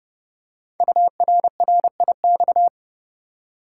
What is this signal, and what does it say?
Morse code, heard as a keyed tone, about 28 wpm, 700 Hz. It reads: URRIX
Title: Code Groups - 5 Letters - 30wpm